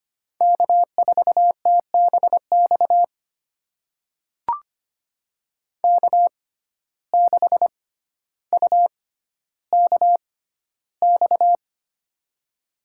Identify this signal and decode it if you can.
Morse code, heard as a keyed tone, about 25 wpm, spaced as for 13 wpm, 700 Hz. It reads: K4TBX K6UKX
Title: USA Call Signs - 5 Characters - 6x Character Spacing - 25wpm